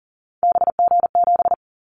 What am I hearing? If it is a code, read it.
Morse code, heard as a keyed tone, about 42 wpm, 700 Hz. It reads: BG7
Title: Callsign Prefixes for Top 100 DXCC - 40wpm